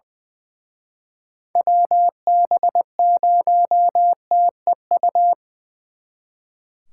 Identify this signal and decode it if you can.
Morse code, heard as a keyed tone, about 23 wpm, 700 Hz. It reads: WB0TEU
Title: Speed Racing - USA Call Signs 20wpm